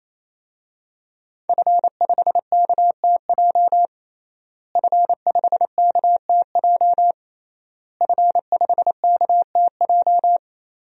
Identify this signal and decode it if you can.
Morse code, heard as a keyed tone, about 27 wpm, 700 Hz. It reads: F5KTJ F5KTJ F5KTJ
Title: DXCC-100 Call Signs - 5 Characters - Repeated 3x - 28wpm